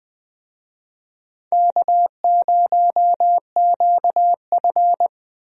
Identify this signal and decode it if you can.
Morse code, heard as a keyed tone, about 20 wpm, 700 Hz. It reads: K0QF
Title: USA Call Signs - 4 Characters - Repeated 2x - 20wpm